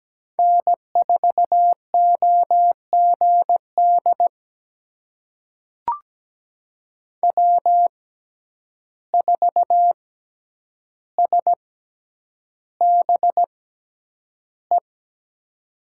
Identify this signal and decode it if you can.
Morse code, heard as a keyed tone, about 17 wpm, spaced as for 9 wpm, 700 Hz. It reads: N4OGD W4SBE
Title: USA Call Signs - 6x Character Spacing - 17wpm